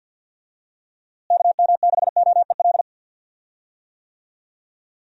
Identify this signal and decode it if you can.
Morse code, heard as a keyed tone, about 52 wpm, 700 Hz. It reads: KM6OEL
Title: USA Call Signs - 6 Characters - Repeated 3x - 50wpm